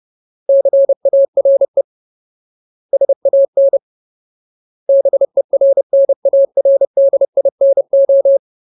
Morse code, heard as a keyed tone, about 30 wpm, 550 Hz. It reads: CARE SAN BERNARDINO
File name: Ultimate Word List / 30WPM Ultimate